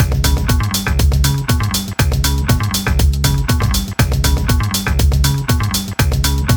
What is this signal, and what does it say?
There is no code.